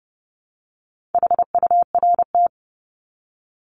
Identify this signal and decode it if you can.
Morse code, heard as a keyed tone, about 30 wpm, 700 Hz. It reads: HURT
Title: Top 1751-2000 Words 30wpm